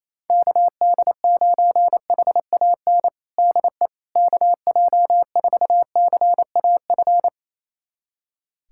Morse code, heard as a keyed tone, about 28 wpm, 700 Hz. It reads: KD9HAN DE KJ4CAF
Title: USA Callsign DE Callsign 28wpm